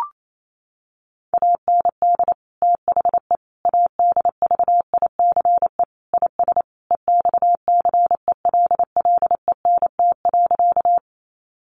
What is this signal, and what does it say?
Morse code, heard as a keyed tone, about 28 wpm, 700 Hz. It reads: AND THE ADVICE IS EXCELLENT.
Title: Sentences from Top 1100 Words 28wpm